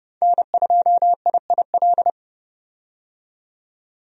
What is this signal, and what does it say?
Morse code, heard as a keyed tone, about 30 wpm, 700 Hz. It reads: N2IIL
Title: USA Call Signs - 5 Characters - 30wpm